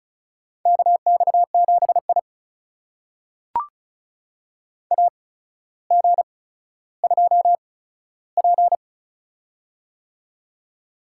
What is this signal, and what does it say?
Morse code, heard as a keyed tone, about 36 wpm, spaced as for 15 wpm, 700 Hz. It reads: KX7I AG2P
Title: USA Call Signs - 4 Characters - 8x Character Spacing - 35wpm